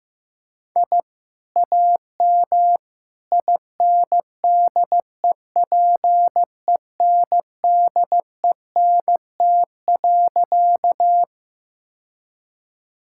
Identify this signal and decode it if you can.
Morse code, heard as a keyed tone, about 16 wpm, 700 Hz. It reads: I AM INDEPENDENT.
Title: Sentences from Top 1000 Words - Review Part 6 - 15wpm